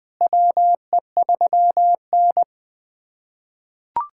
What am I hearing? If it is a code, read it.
Morse code, heard as a keyed tone, about 20 wpm, 700 Hz. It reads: WE3N
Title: USA Callsign DE Callsign - 4 Characters - 20wpm